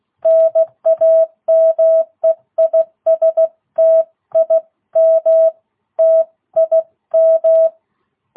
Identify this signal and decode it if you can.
Morse code, heard as a keyed tone, about 16 wpm, 660 Hz. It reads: NAME IS TIM TIM
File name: jh8cbh / 2022cw